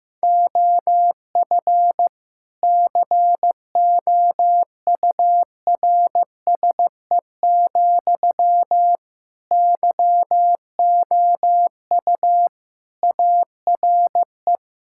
Morse code, heard as a keyed tone, about 16 wpm, 700 Hz. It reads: OF COURSE, YOU ARE
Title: Sentences from Top 1000 Words - Review Part 6 - 15wpm